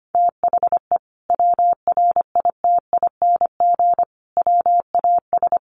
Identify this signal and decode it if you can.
Morse code, heard as a keyed tone, about 25 wpm, 700 Hz. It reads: THE WRITING WAS
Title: Sentences from Top 1100 Words 25wpm